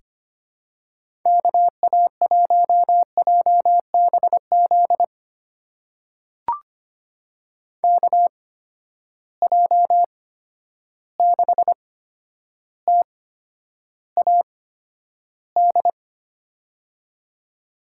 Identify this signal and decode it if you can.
Morse code, heard as a keyed tone, about 25 wpm, spaced as for 11 wpm, 700 Hz. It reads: KA1JBZ KJ6TAD
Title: USA Call Signs - 6 Characters - 8x Character Spacing - 25wpm